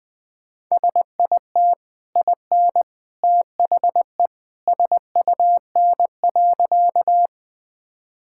Morse code, heard as a keyed tone, about 21 wpm, 700 Hz. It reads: SIT IN THE SUN.
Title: Easy Sentences with Hints - 20wpm